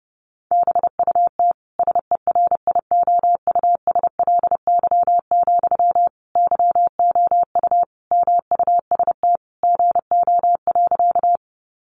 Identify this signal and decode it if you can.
Morse code, heard as a keyed tone, about 30 wpm, 700 Hz. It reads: BUT SERIOUSLY, YOU MUST GO.
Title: Sentences from Top 1200 Words 30wpm